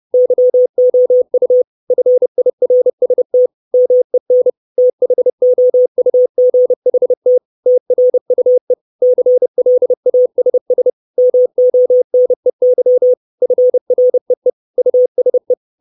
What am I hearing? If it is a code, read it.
Morse code, heard as a keyed tone, about 30 wpm, 500 Hz. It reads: YOU FIRST MEN THOUGHT TRUE CLASS MONEY FREE USE
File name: 632 Words 30WPM